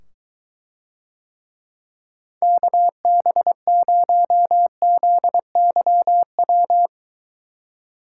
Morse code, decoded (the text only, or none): KB0ZYW